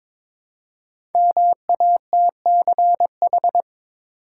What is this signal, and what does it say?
Morse code, heard as a keyed tone, about 22 wpm, 700 Hz. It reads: MATCH
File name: 5 Letter Words - First 3 Letters Given - 22wpm